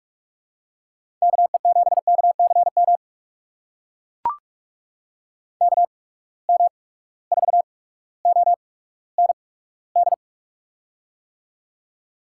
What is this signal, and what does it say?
Morse code, heard as a keyed tone, about 43 wpm, spaced as for 19 wpm, 700 Hz. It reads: KE7KCM KM4OND
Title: USA Call Signs - 6 Characters - 8x Character Spacing - 45wpm